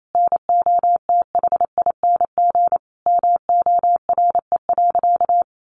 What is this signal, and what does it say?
Morse code, heard as a keyed tone, about 28 wpm, 700 Hz. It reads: NOTHING MORE.
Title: Sentences from Top 800 Words 28wpm